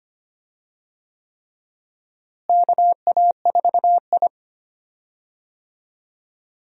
Morse code, heard as a keyed tone, about 25 wpm, 700 Hz. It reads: KA4I
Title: USA Call Signs - 4 Characters - Repeated 3x - 25wpm